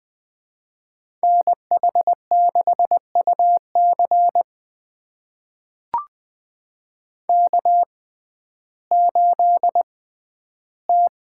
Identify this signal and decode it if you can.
Morse code, heard as a keyed tone, about 20 wpm, spaced as for 12 wpm, 700 Hz. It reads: NH6UC K8T
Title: USA Call Signs - 6x Character Spacing - 20wpm